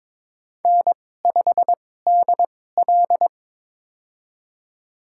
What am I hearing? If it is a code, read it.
Morse code, heard as a keyed tone, about 22 wpm, spaced as for 16 wpm, 700 Hz. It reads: N5DL
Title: USA Call Signs - 4 Characters - 2x Character Spacing - 22wpm